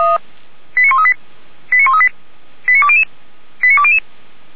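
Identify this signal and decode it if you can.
Five-tone selective call (ZVEI), heard as quick runs of five tones, tones of about 70 ms. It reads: 89138 89138 89200 89200
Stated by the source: ZVEI five-tone selective call received on an SDR, from the audio archive of Feuerwehr Orlamünde.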